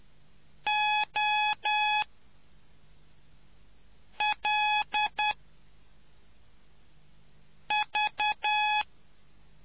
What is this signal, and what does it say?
Morse code, heard as a keyed tone, about 10 wpm, spaced as for 3 wpm, 840 Hz. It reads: OL V